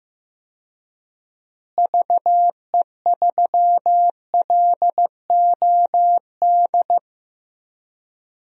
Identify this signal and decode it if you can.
Morse code, heard as a keyed tone, about 15 wpm, 700 Hz. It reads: VE3LOD
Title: DXCC-100 Call Signs - 6 Characters - Repeated 2x - 15wpm